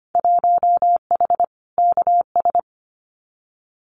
Morse code, heard as a keyed tone, about 25 wpm, 700 Hz. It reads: 1H KS
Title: Winter Field Day Exchanges 25wpm